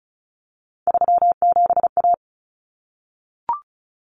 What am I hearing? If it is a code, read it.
Morse code, heard as a keyed tone, about 34 wpm, 700 Hz. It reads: 37A
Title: Common Cut Numbers - 3-Digit 35wpm